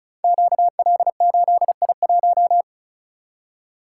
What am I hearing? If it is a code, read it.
Morse code, heard as a keyed tone, about 36 wpm, 700 Hz. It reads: QL8I1
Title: Code Groups - 5 Letter:Numbers - 35wpm